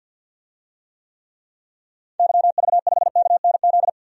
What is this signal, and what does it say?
Morse code, heard as a keyed tone, about 51 wpm, 700 Hz. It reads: YV5KNZ
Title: DXCC-100 Call Signs - 6 Characters - Repeated 3x - 50wpm